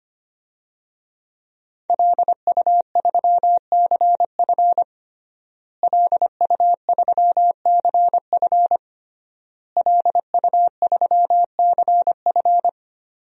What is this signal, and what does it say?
Morse code, heard as a keyed tone, about 25 wpm, 700 Hz. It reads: LU3CF LU3CF LU3CF
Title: DXCC-100 Call Signs - 5 Characters - Repeated 3x - 25wpm